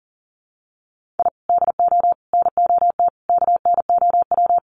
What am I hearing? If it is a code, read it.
Morse code, heard as a keyed tone, about 40 wpm, 700 Hz. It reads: I DO NOT KNOW
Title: Sets of 5 Words Encore 40wpm